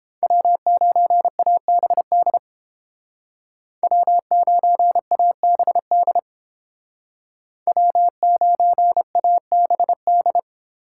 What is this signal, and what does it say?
Morse code, heard as a keyed tone, about 29 wpm, 700 Hz. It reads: W9ABD W9ABD W9ABD